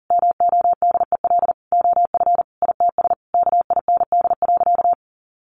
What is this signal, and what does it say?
Morse code, heard as a keyed tone, about 40 wpm, 700 Hz. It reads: MODEL OF ITS KIND.
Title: Sentences from Top 1000 Words 40wpm